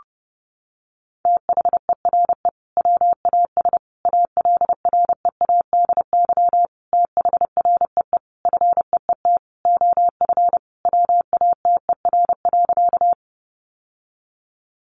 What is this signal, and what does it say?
Morse code, heard as a keyed tone, about 30 wpm, 700 Hz. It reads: THERE WAS ALREADY THREE FEET OF WATER.
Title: Sentences from Top 700 Words 30wpm